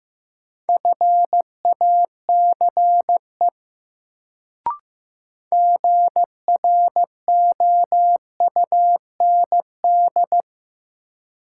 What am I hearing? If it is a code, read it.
Morse code, heard as a keyed tone, about 15 wpm, 700 Hz. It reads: FACE GROUND
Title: Sets of 3 Words - New Word Drill - 15wpm